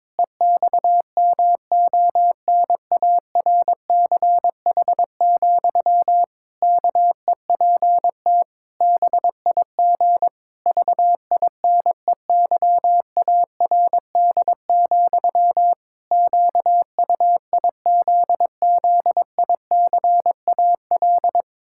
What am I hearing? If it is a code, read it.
Morse code, heard as a keyed tone, about 22 wpm, 700 Hz. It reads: EXMONARCH, KEPT BIG VINEYARD, QUIZZICAL
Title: Pangram Challenge v2 - 22wpm